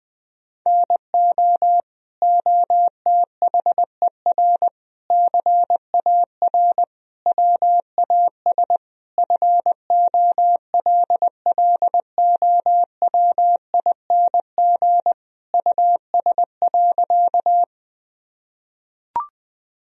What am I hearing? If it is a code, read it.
Morse code, heard as a keyed tone, about 21 wpm, 700 Hz. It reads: NO OTHER CAR WAS FOLLOWING US.
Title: Sentences from Top 1000 Words - Review Part 4 - 20wpm